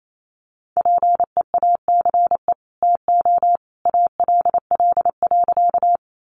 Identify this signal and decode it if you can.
Morse code, heard as a keyed tone, about 28 wpm, 700 Hz. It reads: PEACE TO ALL.
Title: Sentences from Top 1500 Words 28wpm